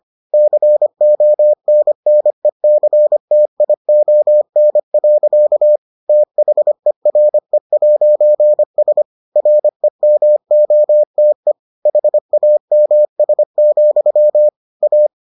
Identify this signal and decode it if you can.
Morse code, heard as a keyed tone, about 25 wpm, 600 Hz. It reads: CONNECTION. THERE'S REMOTE HAMS, A